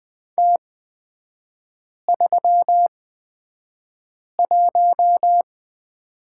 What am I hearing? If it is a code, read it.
Morse code, heard as a keyed tone, about 20 wpm, spaced as for 5 wpm, 700 Hz. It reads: T31
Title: 250 Lesson 40e - BK - Three Characters 20wpm